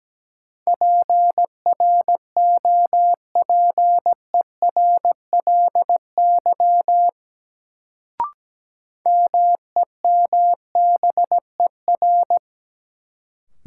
Morse code, heard as a keyed tone, about 17 wpm, 700 Hz. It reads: PROPERLY MEMBER
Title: Sentences from Top 1100 Words - New Word Drill Alt - 17wpm